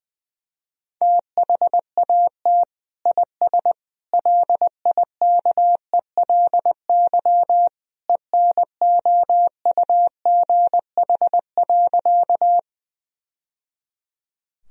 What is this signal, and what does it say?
Morse code, heard as a keyed tone, about 20 wpm, 700 Hz. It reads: THAT IS LIKELY ENOUGH.
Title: Sentences from Top 1000 Words - Review Part 4 - 20wpm